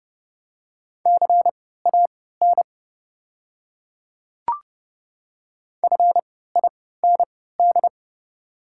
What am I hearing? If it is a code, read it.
Morse code, heard as a keyed tone, about 32 wpm, spaced as for 17 wpm, 700 Hz. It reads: CAN FIND
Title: 3x Character Spacing - Top 200 Words - 30wpm